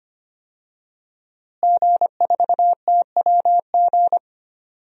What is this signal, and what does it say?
Morse code, heard as a keyed tone, about 25 wpm, 700 Hz. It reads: G4TWG
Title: DXCC-100 Call Signs - 5 Characters - Repeated 2x - 25wpm